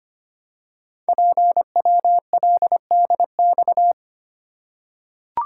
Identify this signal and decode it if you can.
Morse code, heard as a keyed tone, about 25 wpm, 700 Hz. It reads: PWLDX
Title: Code Groups - 5 Letter:Numbers - 25wpm